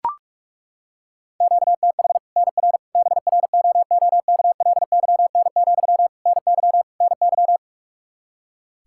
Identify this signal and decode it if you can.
Morse code, heard as a keyed tone, about 41 wpm, 700 Hz. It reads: QTH NR BROOKLYN, NY NY
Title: Practice QSO - Exchange 1 - Beginning - 45wpm